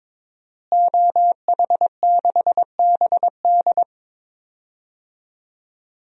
Morse code, heard as a keyed tone, about 22 wpm, 700 Hz. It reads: OH6BD
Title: DXCC-100 Call Signs - 5 Characters - Repeated 3x - 22wpm